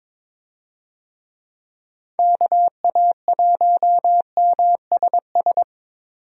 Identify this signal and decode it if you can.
Morse code, heard as a keyed tone, about 22 wpm, 700 Hz. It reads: KA1MSS
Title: USA Call Signs - 6 Characters - 22wpm